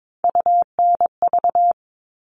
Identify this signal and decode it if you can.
Morse code, heard as a keyed tone, about 22 wpm, 700 Hz. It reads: UNV
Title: All Cut Numbers - 3-Digit 22wpm